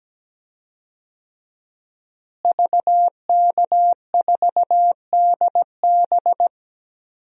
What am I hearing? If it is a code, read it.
Morse code, heard as a keyed tone, about 17 wpm, 700 Hz. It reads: VK4DB
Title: DXCC-100 Call Signs - 5 Characters - Repeated 3x - 17wpm